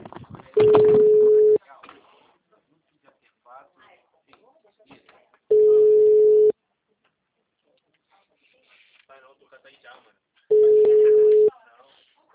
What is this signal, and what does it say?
Ringback tone (Europe, 425 Hz), with speech.